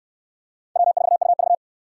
Morse code, heard as a keyed tone, about 62 wpm, 700 Hz. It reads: W4UV